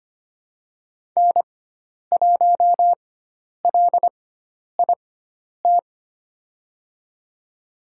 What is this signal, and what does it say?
Morse code, heard as a keyed tone, about 25 wpm, spaced as for 10 wpm, 700 Hz. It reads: N1LIT
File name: Lesson 16g - Letter c - Call Signs 25wpm